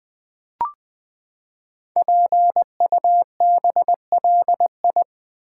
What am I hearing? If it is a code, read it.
Morse code, heard as a keyed tone, about 20 wpm, 700 Hz. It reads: PUBLI
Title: Build A Word - 5 Letters At A Time - 10-Letter Words - 20wpm